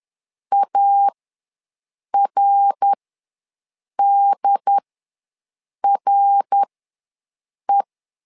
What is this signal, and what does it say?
Morse code, heard as a keyed tone, about 11 wpm, spaced as for 6 wpm, 800 Hz. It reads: ARDRE
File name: LEZIONI / 3 MWRO / 3 MWR ws 10wpm 2151